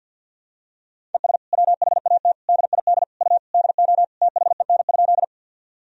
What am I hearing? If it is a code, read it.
Morse code, heard as a keyed tone, about 52 wpm, 700 Hz. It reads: ES WHAT DID U DO THEN?
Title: Sentences from Top 100 Words and Abbreviations 50wpm